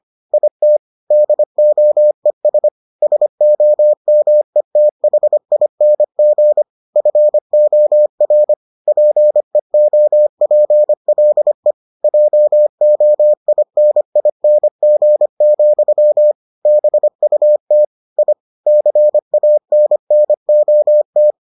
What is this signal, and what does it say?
Morse code, heard as a keyed tone, about 25 wpm, 600 Hz. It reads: IT DOES SOMETHING FOR PEOPLE JOINING, BUT I CANNOT